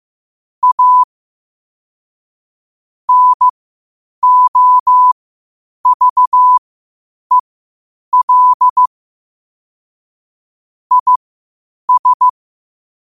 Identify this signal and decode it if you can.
Morse code, heard as a keyed tone, about 15 wpm, spaced as for 8 wpm, 1000 Hz. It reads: A NOVEL IS
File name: qotdfiles / 8wpm